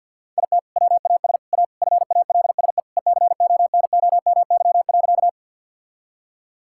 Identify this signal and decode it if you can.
Morse code, heard as a keyed tone, about 50 wpm, 700 Hz. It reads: IT WAS A FALSE ECONOMY.
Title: Sentences from Top 1600 Words 50wpm